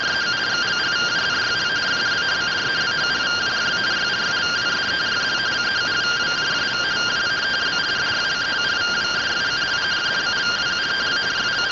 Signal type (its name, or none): two-tone teleprinter signal